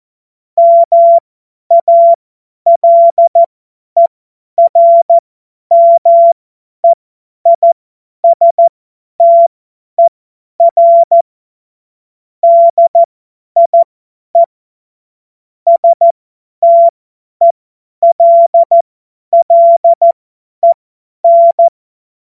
Morse code, heard as a keyed tone, about 14 wpm, spaced as for 10 wpm, 680 Hz. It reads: MALERMEISTER DIE STELLEN
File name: Morsekurs / Witze / Witz 23 Januar 2024 50 F70